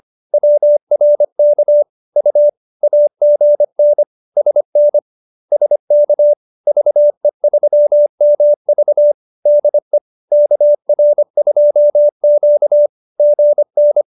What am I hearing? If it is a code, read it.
Morse code, heard as a keyed tone, about 25 wpm, 590 Hz. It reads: WRK U AGN SN SK VE3MV DE KR2Q GN